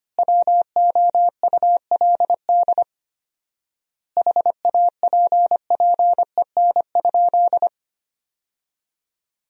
Morse code, heard as a keyed tone, about 25 wpm, 700 Hz. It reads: WOULD HAPPEN?
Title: 4x Word-Spacing - Sentences from Top 400 Words - 25wpm